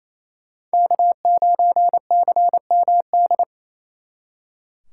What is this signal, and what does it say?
Morse code, heard as a keyed tone, about 35 wpm, 700 Hz. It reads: K9CMD